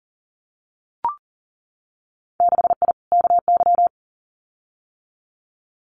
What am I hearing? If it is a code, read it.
Morse code, heard as a keyed tone, about 40 wpm, 700 Hz. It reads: 6I KY
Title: Winter Field Day Exchanges 40wpm